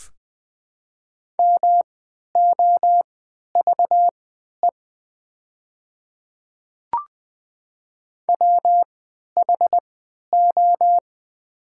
Morse code, heard as a keyed tone, about 20 wpm, spaced as for 11 wpm, 700 Hz. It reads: MOVE WHO
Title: 3x Character Spacing - Top 200 Words - 20wpm